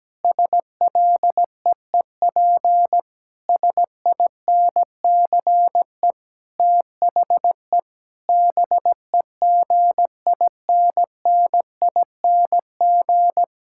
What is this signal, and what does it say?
Morse code, heard as a keyed tone, about 17 wpm, 700 Hz. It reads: SLEEP SINCE THE BEGINNING
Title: Sentences from Top 1000 Words - Review Part 6 - 17wpm